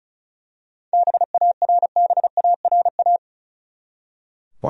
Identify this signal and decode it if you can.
Morse code, heard as a keyed tone, about 34 wpm, 700 Hz. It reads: BARBARA
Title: USA Common Names - 35wpm